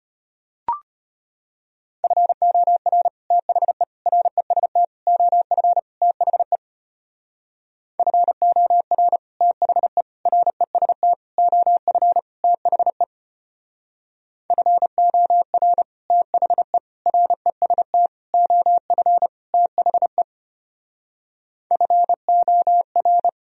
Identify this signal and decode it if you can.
Morse code, heard as a keyed tone, about 30 wpm, 700 Hz. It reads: FOR THE REST OF THE FOR THE REST OF THE FOR THE REST OF THE FOR
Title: Speed-Racing - Sets of 5 Words 25wpm